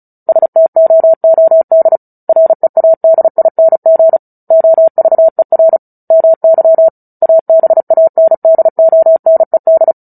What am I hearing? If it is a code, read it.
Morse code, heard as a keyed tone, about 34 wpm, 650 Hz. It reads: STOOD READING OVER MY ABANDONED
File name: Book Two / 09-Chapter 9 - 35 WPM